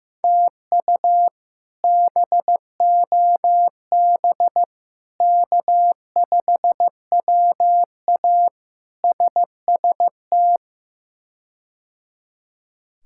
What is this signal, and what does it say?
Morse code, heard as a keyed tone, about 15 wpm, 700 Hz. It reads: TU BOB K5WA SST
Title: K1USN SST QSO Practice - 15wpm